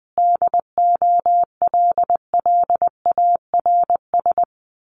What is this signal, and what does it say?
Morse code, heard as a keyed tone, about 20 wpm, 700 Hz. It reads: DOLLARS
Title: Sets of 5 Words Encore 20wpm